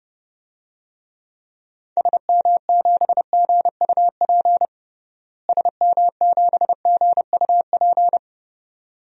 Morse code, heard as a keyed tone, about 30 wpm, 700 Hz. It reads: SM7GUP SM7GUP
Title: DXCC-100 Call Signs - 6 Characters - Repeated 2x - 30wpm